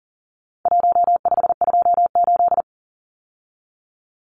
Morse code, heard as a keyed tone, about 39 wpm, 700 Hz. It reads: 1528
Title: Numbers - Contest Style - 4-Digit 40wpm